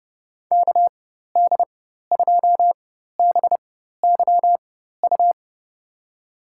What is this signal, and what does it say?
Morse code, heard as a keyed tone, about 30 wpm, spaced as for 14 wpm, 700 Hz. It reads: KD2BYU
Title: USA Call Signs - 6 Characters - 4x Character Spacing - 30wpm